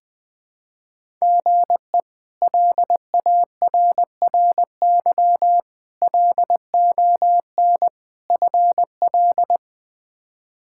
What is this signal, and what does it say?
Morse code, heard as a keyed tone, about 21 wpm, 700 Hz. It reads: GE LARRY LON FL
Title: K1USN SST QSO Practice - 20wpm